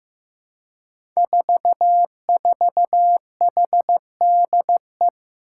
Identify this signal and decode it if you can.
Morse code, heard as a keyed tone, about 15 wpm, 700 Hz. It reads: 44HDE
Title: Five Challenging Letter-Number Combinations 15wpm